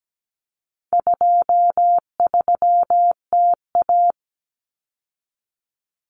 Morse code, heard as a keyed tone, about 17 wpm, 700 Hz. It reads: 23TA
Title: Common Cut Numbers - 4-Digit 17wpm